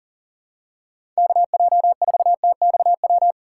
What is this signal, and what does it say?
Morse code, heard as a keyed tone, about 40 wpm, 700 Hz. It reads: KJ4TXW
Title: USA Call Signs - 6 Characters - 6x Character Spacing - 40wpm